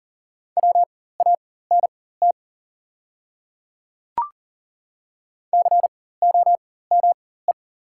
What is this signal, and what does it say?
Morse code, heard as a keyed tone, about 40 wpm, spaced as for 19 wpm, 700 Hz. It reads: WANT COME
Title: 4x Character Spacing - Top 200 Words - 40wpm